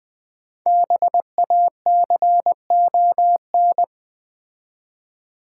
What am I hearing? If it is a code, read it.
Morse code, heard as a keyed tone, about 20 wpm, 690 Hz. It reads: BACON
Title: Difficult Words with Hints - 20wpm